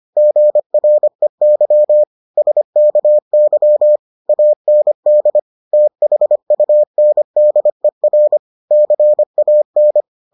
Morse code, heard as a keyed tone, about 25 wpm, 600 Hz. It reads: GREY SKY AND THUNDER CAN